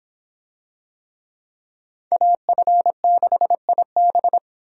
Morse code, heard as a keyed tone, about 26 wpm, 700 Hz. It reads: AF6IB